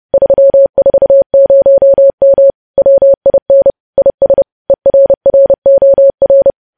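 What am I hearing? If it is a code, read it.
Morse code, heard as a keyed tone, about 30 wpm, 550 Hz. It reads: 340M WIN IS ERROR
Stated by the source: Morse code news headlines practice, preceded by vvv.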